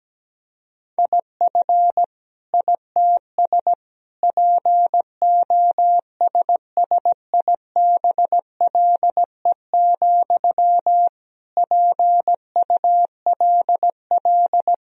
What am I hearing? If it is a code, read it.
Morse code, heard as a keyed tone, about 17 wpm, 700 Hz. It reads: IF ITS POSSIBLE, PULL
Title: Sentences from Top 1000 Words - Review Part 2 - 17wpm